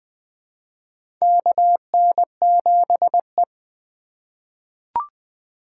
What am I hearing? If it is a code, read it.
Morse code, heard as a keyed tone, about 20 wpm, 700 Hz. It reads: KN7E
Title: USA Call Signs - 4 Characters - 2x Character Spacing - 20wpm